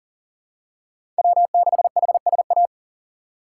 Morse code, heard as a keyed tone, about 39 wpm, 700 Hz. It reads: W6HSA
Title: USA Call Signs - 5 Characters - 2x Character Spacing - 40wpm